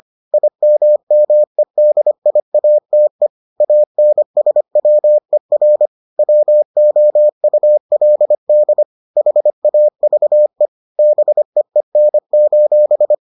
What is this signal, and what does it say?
Morse code, heard as a keyed tone, about 25 wpm, 600 Hz. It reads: IMMEDIATE ANSWER WOULD HAVE BEEN: